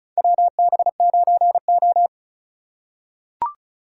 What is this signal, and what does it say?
Morse code, heard as a keyed tone, about 36 wpm, 700 Hz. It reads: WB9O
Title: USA Call Signs - 8x Character Spacing - 35wpm